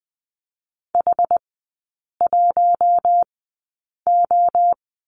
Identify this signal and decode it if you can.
Morse code, heard as a keyed tone, about 20 wpm, spaced as for 8 wpm, 690 Hz. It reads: H1O
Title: 060 Lesson 11e - Letter h - Three Characters 20wpm